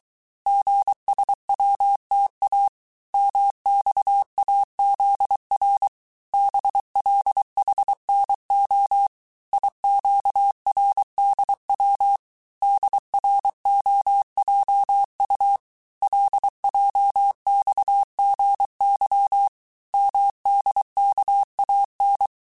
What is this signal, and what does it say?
Morse code, heard as a keyed tone, about 23 wpm, 800 Hz. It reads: GSWTA MXAZR BLHNO IQRDW DROJU LJXGY MDKAN
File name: cours-du-mercredi / audios / cours-855-texte-chiffre-vitesse-22-mpm